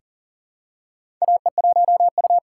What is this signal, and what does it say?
Morse code, heard as a keyed tone, about 41 wpm, 700 Hz. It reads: AE1U
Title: USA Call Signs - 3x Character Spacing - 40wpm